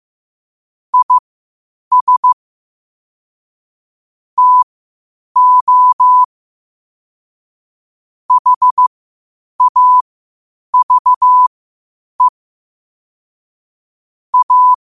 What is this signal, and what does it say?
Morse code, heard as a keyed tone, about 15 wpm, spaced as for 8 wpm, 1000 Hz. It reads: IS TO HAVE A
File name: qotdfiles / 8wpm